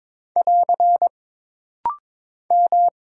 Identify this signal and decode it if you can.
Morse code, heard as a keyed tone, about 22 wpm, 700 Hz. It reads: <AR> M
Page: This rapid-fire timing is geared to push your ICR (Instant Character Recognition) skills.